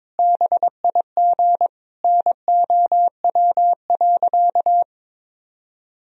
Morse code, heard as a keyed tone, about 22 wpm, 700 Hz. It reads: BIG NOW.
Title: Sentences from Top 1000 Words - Review Part 1 - 22wpm